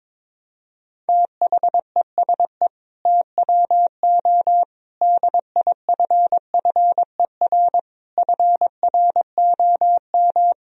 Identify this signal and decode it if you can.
Morse code, heard as a keyed tone, about 22 wpm, 700 Hz. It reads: THESE TWO DIFFER FROM
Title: Sentences from Top 1000 Words - Review Part 1 - 22wpm